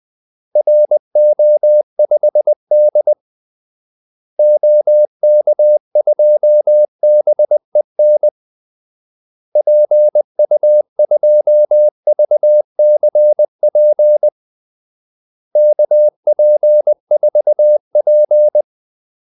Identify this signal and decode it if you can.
Morse code, heard as a keyed tone, about 20 wpm, 600 Hz. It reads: RO5D OK2BEN PU2VCP KP4P
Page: Kaldesignaler 20 wpm